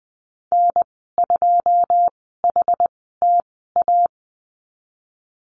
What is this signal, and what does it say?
Morse code, heard as a keyed tone, about 20 wpm, spaced as for 15 wpm, 700 Hz. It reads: N2HTA
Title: 082 Lesson 14g - Number 2 - Call Signs 20wpm